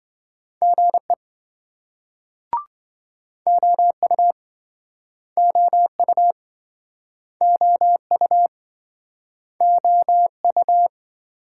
Morse code, heard as a keyed tone, about 28 wpm, 700 Hz. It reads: GE OU OU OU OU